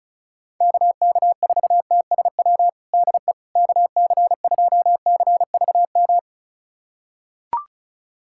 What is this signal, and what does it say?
Morse code, heard as a keyed tone, about 36 wpm, 700 Hz. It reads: KK4TSW DE KC2CVM
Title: USA Callsign DE Callsign - 6 Characters - 35wpm